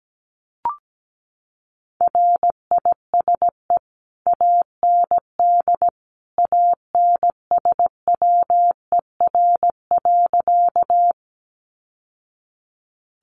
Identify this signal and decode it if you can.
Morse code, heard as a keyed tone, about 17 wpm, 700 Hz. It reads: RISE AND ANSWER.
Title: Sentences from Top 900 Words 17wpm